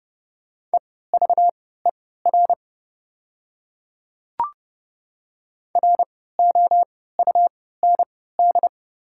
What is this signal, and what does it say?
Morse code, heard as a keyed tone, about 30 wpm, spaced as for 17 wpm, 690 Hz. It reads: EVER ROUND
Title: 3x Character Spacing - Top 500 Words - 30wpm